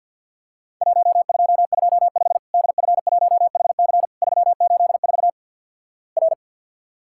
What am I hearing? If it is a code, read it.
Morse code, heard as a keyed tone, about 50 wpm, 700 Hz. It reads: 1225 DF1HC 384 R
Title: Worked All Europe DX Contest - QTC Exchange - 50wpm